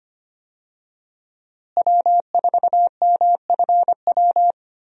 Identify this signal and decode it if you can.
Morse code, heard as a keyed tone, about 25 wpm, 700 Hz. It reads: W4MFW
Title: USA Call Signs - 5 Characters - Repeated 2x - 25wpm